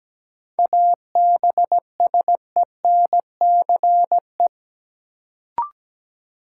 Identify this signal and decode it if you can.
Morse code, heard as a keyed tone, about 17 wpm, 700 Hz. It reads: ABSENCE
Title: Sets of 4 Words - New Word Drill - 17wpm